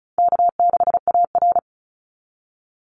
Morse code, heard as a keyed tone, about 34 wpm, 700 Hz. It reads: K6AR
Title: Call Signs - 35wpm